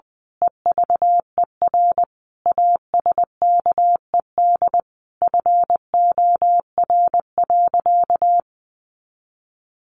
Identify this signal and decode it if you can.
Morse code, heard as a keyed tone, about 20 wpm, 700 Hz. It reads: EVER ASKED FOR.
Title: A Sound Of Thunder 20wpm